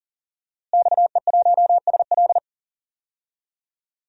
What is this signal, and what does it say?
Morse code, heard as a keyed tone, about 40 wpm, 700 Hz. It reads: XE1SL
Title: DXCC-100 Call Signs - 5 Characters - Repeated 3x - 40wpm